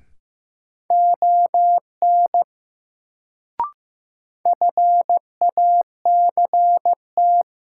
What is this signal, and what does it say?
Morse code, heard as a keyed tone, about 15 wpm, 700 Hz. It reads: ON FACT
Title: Sets of 2 Words - New Word Drill - 15wpm